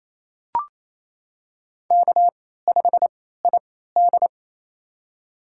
Morse code, heard as a keyed tone, about 27 wpm, spaced as for 16 wpm, 700 Hz. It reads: K5ID
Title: USA Call Signs - 4 Characters - 3x Character Spacing - 28wpm